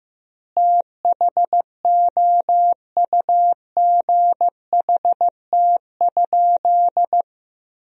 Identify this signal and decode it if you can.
Morse code, heard as a keyed tone, about 15 wpm, 700 Hz. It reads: THOUGHT?
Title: 4x Word-Spacing - Sentences from Top 300 Words - 15wpm